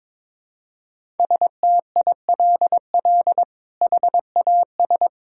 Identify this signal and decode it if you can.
Morse code, heard as a keyed tone, about 21 wpm, 700 Hz. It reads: STILL HAS
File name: practice / 22wpm / Sets of 7 Words - 22wpm